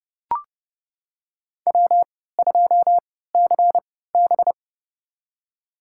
Morse code, heard as a keyed tone, about 30 wpm, spaced as for 17 wpm, 700 Hz. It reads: W2CB
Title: USA Call Signs - 3x Character Spacing - 30wpm